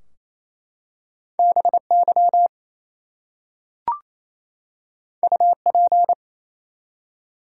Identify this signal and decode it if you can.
Morse code, heard as a keyed tone, about 27 wpm, 700 Hz. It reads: BY UP
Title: Two Letter Words - 28wpm